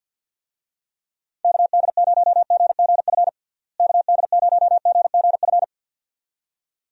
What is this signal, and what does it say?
Morse code, heard as a keyed tone, about 47 wpm, 700 Hz. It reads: KD0GGF KD0GGF
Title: USA Call Signs - 6 Characters - Repeated 2x - 50wpm